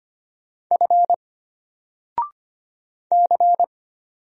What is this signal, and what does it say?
Morse code, heard as a keyed tone, about 25 wpm, 700 Hz. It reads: F C
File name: Single-Letter-Number-25wpm